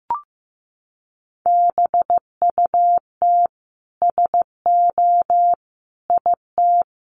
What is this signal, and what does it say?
Morse code, heard as a keyed tone, about 15 wpm, 700 Hz. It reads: BUT SO IT
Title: Sentences from Top 100 Words 15wpm